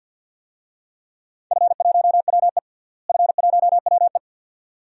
Morse code, heard as a keyed tone, about 52 wpm, 700 Hz. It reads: F1WE F1WE
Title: DXCC-100 Call Signs - 4 Characters - Repeated 2x - 50wpm